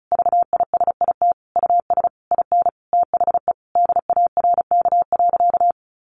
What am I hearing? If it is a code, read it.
Morse code, heard as a keyed tone, about 34 wpm, 700 Hz. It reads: VISIT US IN THE DARK.
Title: Sentences from Top 1400 Words 35wpm